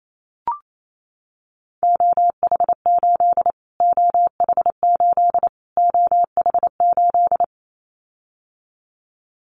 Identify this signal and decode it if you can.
Morse code, heard as a keyed tone, about 28 wpm, 700 Hz. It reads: OH8 OH8 OH8
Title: Callsign Prefixes for Top 25 DXCC - Repeated 3x - 28wpm